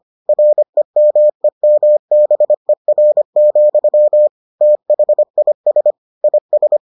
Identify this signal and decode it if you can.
Morse code, heard as a keyed tone, about 25 wpm, 600 Hz. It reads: REMEMBER, THIS IS